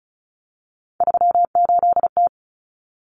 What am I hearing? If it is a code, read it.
Morse code, heard as a keyed tone, about 34 wpm, 700 Hz. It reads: 38T
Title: Common Cut Numbers - 3-Digit 35wpm